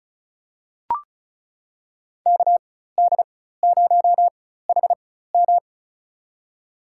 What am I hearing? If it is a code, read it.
Morse code, heard as a keyed tone, about 34 wpm, spaced as for 16 wpm, 700 Hz. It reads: KD0HM
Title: USA Call Signs - 5 Characters - 4x Character Spacing - 35wpm